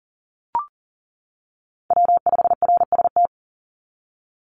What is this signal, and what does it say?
Morse code, heard as a keyed tone, about 40 wpm, 700 Hz. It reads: W5RST
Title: NA SOTA Chaser Callsigns 40wpm